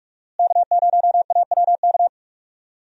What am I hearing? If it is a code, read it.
Morse code, heard as a keyed tone, about 46 wpm, 700 Hz. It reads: K0AWK
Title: USA Call Signs - 5 Characters - Repeated 3x - 45wpm